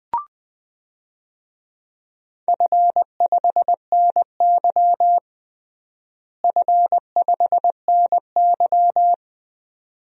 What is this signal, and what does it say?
Morse code, heard as a keyed tone, about 20 wpm, 700 Hz. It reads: F5NY F5NY
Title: DXCC-100 Call Signs - 4 Characters - Repeated 2x - 20wpm